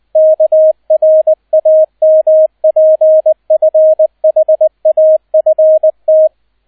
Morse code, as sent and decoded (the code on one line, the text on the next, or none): -.- .-. .- -- .--. ..-. .... .- ..-. -
KRAMPFHAFT